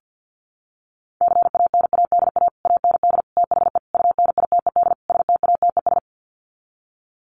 Morse code, heard as a keyed tone, about 49 wpm, 700 Hz. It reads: CANADA AND THE UNITED STATES
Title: Sets of 5 Words Encore 50wpm